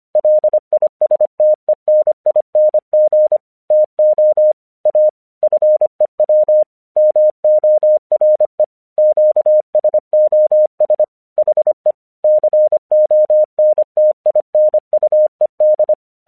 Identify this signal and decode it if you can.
Morse code, heard as a keyed tone, about 25 wpm, 610 Hz. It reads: LISTENING TO A FEW MORE QSOS HE CONTINUED
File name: 1 English 25wpm / AQC29